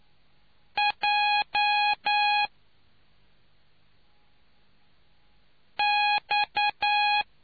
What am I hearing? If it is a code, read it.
Morse code, heard as a keyed tone, about 9 wpm, spaced as for 2 wpm, 830 Hz. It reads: JX